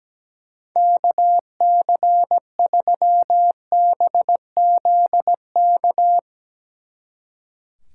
Morse code, heard as a keyed tone, about 18 wpm, 700 Hz. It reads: KC3BZK
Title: Speed Racing - USA 2x3 Call Signs 17wpm